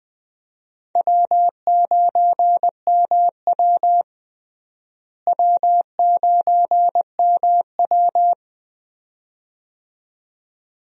Morse code, heard as a keyed tone, about 21 wpm, 700 Hz. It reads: W9MW W9MW
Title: USA Call Signs - 4 Characters - Repeated 2x - 20wpm